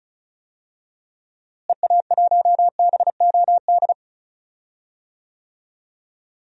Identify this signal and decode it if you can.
Morse code, heard as a keyed tone, about 36 wpm, 700 Hz. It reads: EA1BOD
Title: DXCC-100 Call Signs - 6 Characters - Repeated 2x - 35wpm